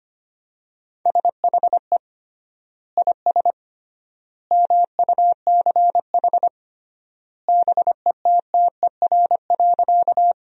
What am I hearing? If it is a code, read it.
Morse code, heard as a keyed tone, about 25 wpm, 700 Hz. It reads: SHE IS MUCH BETTER.
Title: 3x Word-Spacing - Sentences from Top 400 Words - 25wpm